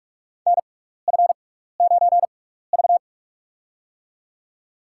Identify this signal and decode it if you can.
Morse code, heard as a keyed tone, about 46 wpm, spaced as for 15 wpm, 700 Hz. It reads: NF9V
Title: USA Call Signs - 4 Characters - 6x Character Spacing - 45wpm